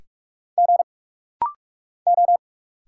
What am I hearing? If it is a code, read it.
Morse code, heard as a keyed tone, about 46 wpm, 700 Hz. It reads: G O
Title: Single Letters - Rapid-Fire 45wpm